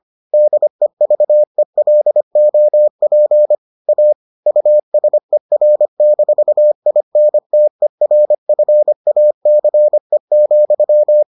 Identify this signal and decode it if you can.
Morse code, heard as a keyed tone, about 25 wpm, 600 Hz. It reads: DEVELOP A USER-INTERFACE,